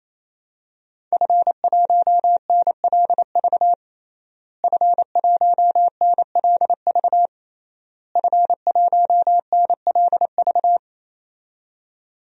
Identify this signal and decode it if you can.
Morse code, heard as a keyed tone, about 28 wpm, 700 Hz. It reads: F1NLV F1NLV F1NLV
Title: DXCC-100 Call Signs - 5 Characters - Repeated 3x - 28wpm